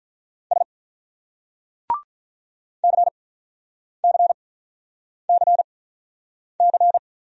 Morse code, heard as a keyed tone, about 46 wpm, spaced as for 9 wpm, 700 Hz. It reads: S CCCC